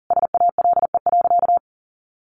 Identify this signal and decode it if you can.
Morse code, heard as a keyed tone, about 40 wpm, 700 Hz. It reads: SALE.
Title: Sentences from Top 1400 Words 40wpm